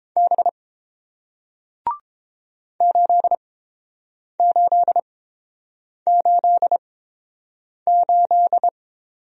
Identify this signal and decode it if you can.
Morse code, heard as a keyed tone, about 30 wpm, spaced as for 7 wpm, 700 Hz. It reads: B 8888